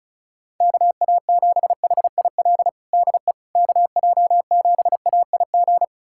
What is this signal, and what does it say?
Morse code, heard as a keyed tone, about 34 wpm, 700 Hz. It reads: KA7HIL DE KJ7AIG
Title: USA Callsign DE Callsign - 6 Characters - 35wpm